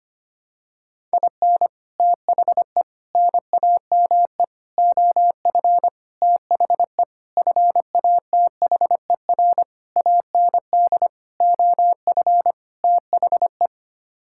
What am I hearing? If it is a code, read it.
Morse code, heard as a keyed tone, about 25 wpm, 700 Hz. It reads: IN THE NAME OF THE FATHER AND OF THE
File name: Sets of 9 Words - 25wpm